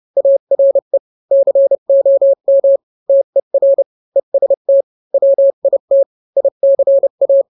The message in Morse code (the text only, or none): ARE COM TER EST WIT ICA